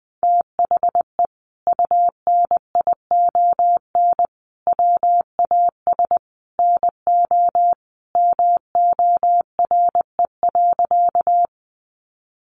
Morse code, heard as a keyed tone, about 20 wpm, 700 Hz. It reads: THE UNION WAS NO MORE.
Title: Sentences from Top 1400 Words 20wpm